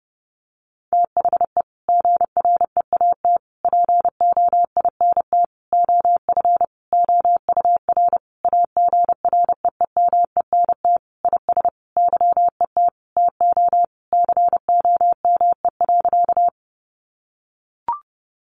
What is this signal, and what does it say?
Morse code, heard as a keyed tone, about 28 wpm, 700 Hz. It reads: THE GREAT POINT OF OUR AGREEMENT IS YET TO COME.
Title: Sentences from Top 1500 Words 30wpm